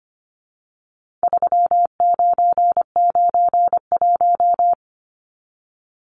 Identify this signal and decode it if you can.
Morse code, heard as a keyed tone, about 25 wpm, 700 Hz. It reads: 3991
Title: Numbers - 4-Digit 25wpm